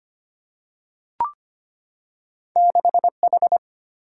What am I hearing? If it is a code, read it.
Morse code, heard as a keyed tone, about 25 wpm, 700 Hz. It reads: 6H
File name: Two Letter-Number Combinations - 25wpm